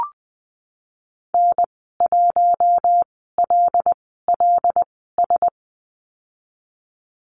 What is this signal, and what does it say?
Morse code, heard as a keyed tone, about 19 wpm, spaced as for 14 wpm, 700 Hz. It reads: N1LLS
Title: 154 Lesson 25g - Letter p - Call Signs 20wpm